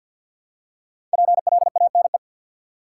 Morse code, heard as a keyed tone, about 53 wpm, 700 Hz. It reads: PLANE
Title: Five Letter Words - 50wpm